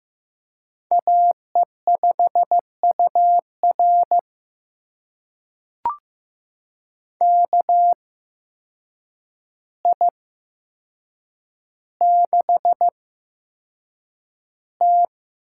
Morse code, heard as a keyed tone, about 15 wpm, spaced as for 7 wpm, 700 Hz. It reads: AE5UR KI6T
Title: USA Call Signs - 5 Characters - 8x Character Spacing - 15wpm